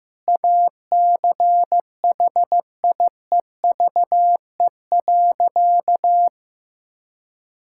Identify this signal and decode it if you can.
Morse code, heard as a keyed tone, about 15 wpm, 700 Hz. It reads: ACHIEVE.
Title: Sentences from Top 1000 Words - Review Part 6 - 15wpm